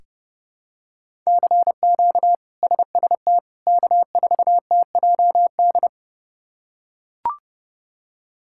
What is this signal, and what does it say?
Morse code, heard as a keyed tone, about 30 wpm, 700 Hz. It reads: CQ SST K4TJD
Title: K1USN SST QSO Practice - 30wpm